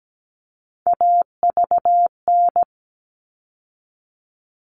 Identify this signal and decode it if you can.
Morse code, heard as a keyed tone, about 17 wpm, 700 Hz. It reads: AVN
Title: All Cut Numbers - 3-Digit 17wpm